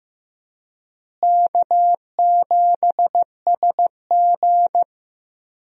Morse code, heard as a keyed tone, about 15 wpm, 700 Hz. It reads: K7SG
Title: USA Call Signs - 4 Characters - Repeated 3x - 15wpm